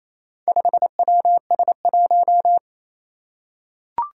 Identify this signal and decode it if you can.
Morse code, heard as a keyed tone, about 27 wpm, 700 Hz. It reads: 5WS1
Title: Four Challenging Letter-Number Combinations 28wpm